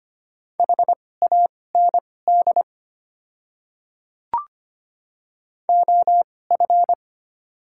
Morse code, heard as a keyed tone, about 25 wpm, spaced as for 18 wpm, 700 Hz. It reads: HAND OF
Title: 2x Character Spacing - Top 300 Words - 25wpm